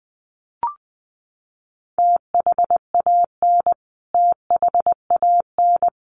Morse code, heard as a keyed tone, about 20 wpm, 700 Hz. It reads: THAN THAN
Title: Top 300 Words Repeated 20wpm